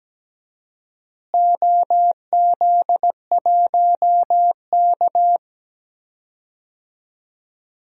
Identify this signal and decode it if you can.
Morse code, heard as a keyed tone, about 17 wpm, 700 Hz. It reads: OZ1K
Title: DXCC-100 Call Signs - 4 Characters - Repeated 2x - 17wpm